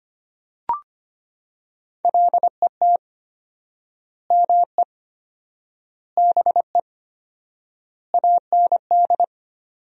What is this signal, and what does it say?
Morse code, heard as a keyed tone, about 25 wpm, 700 Hz. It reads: LET ME BE AND
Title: 4x Word-Spacing - Sentences from Top 300 Words - 25wpm